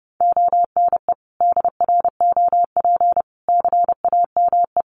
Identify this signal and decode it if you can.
Morse code, heard as a keyed tone, about 30 wpm, 700 Hz. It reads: ONE DROP CAME
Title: Sentences from Top 1400 Words 30wpm